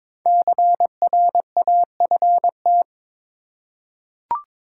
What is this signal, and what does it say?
Morse code, heard as a keyed tone, about 22 wpm, 700 Hz. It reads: CRAFT
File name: 5 Letter Words - First 3 Letters Given - 22wpm